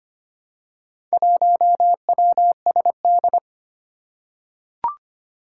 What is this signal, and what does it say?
Morse code, heard as a keyed tone, about 25 wpm, 700 Hz. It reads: 1WSD
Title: Four Challenging Letter-Number Combinations 25wpm